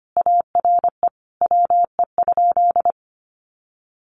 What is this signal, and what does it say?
Morse code, heard as a keyed tone, about 25 wpm, 700 Hz. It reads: ARE WE?
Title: Sentences from Top 1200 Words 25wpm